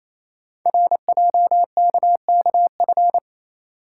Morse code, heard as a keyed tone, about 29 wpm, 700 Hz. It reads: RJKKF
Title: Code Groups - 5 Letters - 28wpm